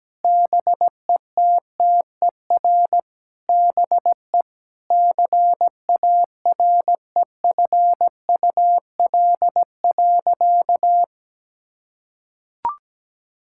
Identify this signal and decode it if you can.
Morse code, heard as a keyed tone, about 17 wpm, 700 Hz. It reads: BETTER BE CAREFUL.
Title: Sentences from Top 1600 Words 17wpm